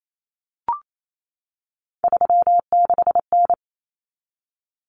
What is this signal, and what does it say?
Morse code, heard as a keyed tone, about 28 wpm, 700 Hz. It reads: 36N
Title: Common Cut Numbers - 3-Digit 28wpm